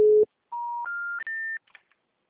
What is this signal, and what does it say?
Special information tone.